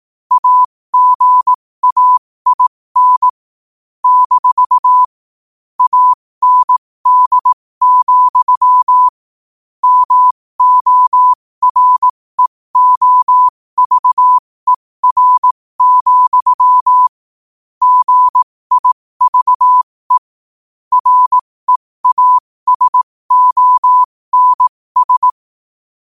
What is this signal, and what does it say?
Morse code, heard as a keyed tone, about 18 wpm, spaced as for 15 wpm, 1000 Hz. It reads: AGAIN - AND, MOREOVER, GIVE REASONS
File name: qotdfiles / 15wpm